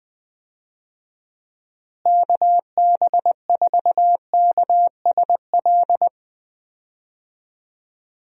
Morse code, heard as a keyed tone, about 21 wpm, 700 Hz. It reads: KB4KSL
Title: USA Call Signs - 6 Characters - Repeated 3x - 20wpm